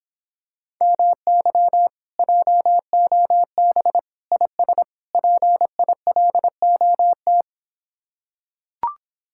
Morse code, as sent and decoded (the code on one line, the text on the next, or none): -- -.-- / .--- --- -... / .. ... / .--. .. .-.. --- -
MY JOB IS PILOT